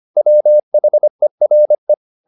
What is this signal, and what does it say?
Morse code, heard as a keyed tone, about 25 wpm, 600 Hz. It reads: WHERE